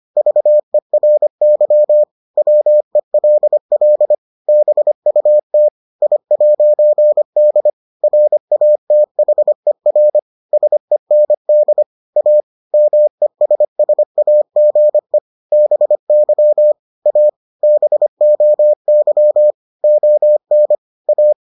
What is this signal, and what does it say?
Morse code, heard as a keyed tone, about 25 wpm, 600 Hz. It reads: VERY WELL BUT I'D RATHER SEND A MESSAGE BY A BOY ON A